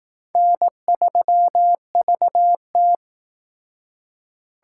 Morse code, heard as a keyed tone, about 19 wpm, 700 Hz. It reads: N3VT
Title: Speed Racing - USA 1x2 Call Signs 15wpm